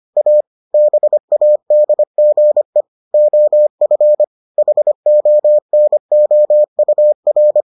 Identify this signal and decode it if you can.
Morse code, heard as a keyed tone, about 25 wpm, 600 Hz. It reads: A BADGE OF HONOUR